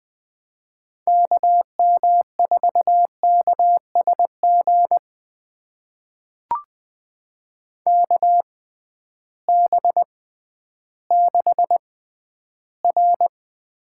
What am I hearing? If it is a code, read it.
Morse code, heard as a keyed tone, about 20 wpm, spaced as for 12 wpm, 700 Hz. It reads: KM4KSG KB6R
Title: USA Call Signs - 6 Characters - 6x Character Spacing - 20wpm